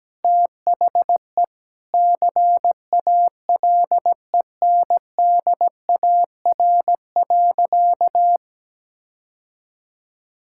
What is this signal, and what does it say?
Morse code, heard as a keyed tone, about 17 wpm, 700 Hz. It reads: THE CALENDAR.
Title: Sentences from Top 1600 Words 17wpm